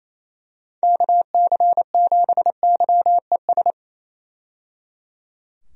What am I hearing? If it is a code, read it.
Morse code, heard as a keyed tone, about 28 wpm, 700 Hz. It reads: KC7YES
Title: USA Call Signs - 6 Characters - 28wpm